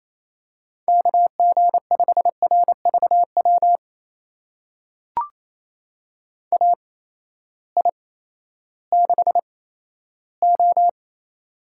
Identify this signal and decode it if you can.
Morse code, heard as a keyed tone, about 28 wpm, spaced as for 14 wpm, 700 Hz. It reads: KG5RVW AI6O
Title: USA Call Signs - 8x Character Spacing - 28wpm